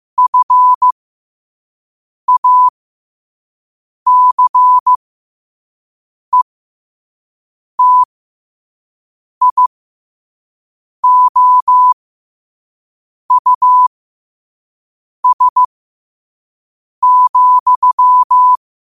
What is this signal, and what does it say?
Morse code, heard as a keyed tone, about 15 wpm, spaced as for 5 wpm, 1000 Hz. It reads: FACETIOUS,
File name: 5wpm